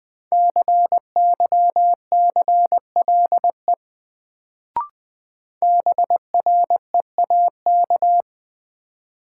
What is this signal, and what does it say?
Morse code, heard as a keyed tone, about 21 wpm, 700 Hz. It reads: CYCLE BREAK
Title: Sentences from Top 900 Words - New Word Drill - 20wpm